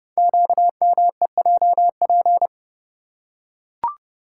Morse code, heard as a keyed tone, about 30 wpm, 700 Hz. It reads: QMEJP
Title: Code Groups - 5 Letter:Numbers - 30wpm